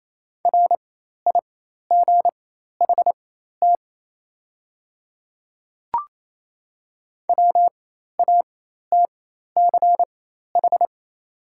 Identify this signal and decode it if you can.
Morse code, heard as a keyed tone, about 28 wpm, spaced as for 13 wpm, 700 Hz. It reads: RIGHT WATCH
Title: 4x Character Spacing - Top 300 Words - 28wpm